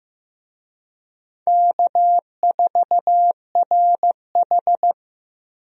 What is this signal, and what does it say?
Morse code, heard as a keyed tone, about 15 wpm, 700 Hz. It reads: K4RH
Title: USA Call Signs - 4 Characters - Repeated 2x - 15wpm